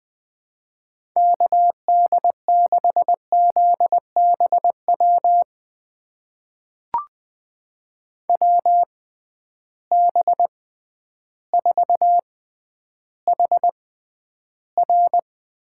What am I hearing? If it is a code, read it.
Morse code, heard as a keyed tone, about 20 wpm, spaced as for 11 wpm, 700 Hz. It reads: KD6ZBW WB4HR
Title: USA Call Signs - 6 Characters - 6x Character Spacing - 20wpm